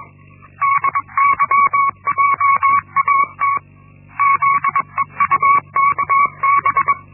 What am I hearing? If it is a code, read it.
Morse code, heard as a keyed tone, about 21 wpm, 1120 Hz. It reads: DYJAT ZEUKB
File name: Linksmakalnis-1